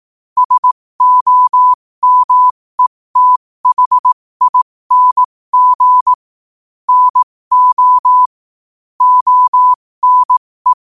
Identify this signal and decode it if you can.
Morse code, heard as a keyed tone, about 18 wpm, spaced as for 15 wpm, 1000 Hz. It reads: SOMETHING NO ONE
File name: qotdfiles / 15wpm